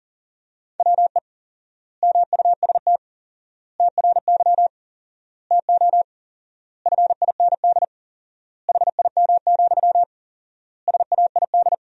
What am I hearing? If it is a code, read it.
Morse code, heard as a keyed tone, about 40 wpm, 700 Hz. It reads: WE MUST TRY TO FIND HIM, SAID
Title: 4x Word-Spacing - Sentences from Top 400 Words - 40wpm